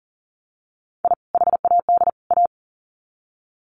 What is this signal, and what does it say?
Morse code, heard as a keyed tone, about 40 wpm, 700 Hz. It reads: I HAD A
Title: Sets of 3 Words 40wpm